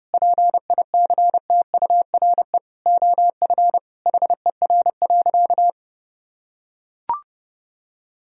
Morse code, heard as a keyed tone, about 32 wpm, 700 Hz. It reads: PICTURE OF HER.
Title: Sentences from Top 1000 Words - Review Part 1 - 30wpm